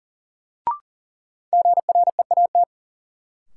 Morse code, heard as a keyed tone, about 40 wpm, 700 Hz. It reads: GREAT
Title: Sets of 2 Words - New Word Drill - 40wpm